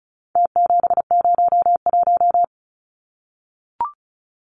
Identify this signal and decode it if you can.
Morse code, heard as a keyed tone, about 34 wpm, 700 Hz. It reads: T701
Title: Numbers - Contest Style - 4-Digit 35wpm